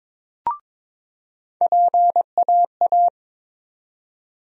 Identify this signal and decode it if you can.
Morse code, heard as a keyed tone, about 22 wpm, 700 Hz. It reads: PAA
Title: USA Call Sign Postfixes - 3 Letters 22wpm